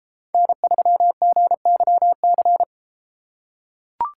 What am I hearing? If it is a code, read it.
Morse code, heard as a keyed tone, about 32 wpm, 700 Hz. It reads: N3GYC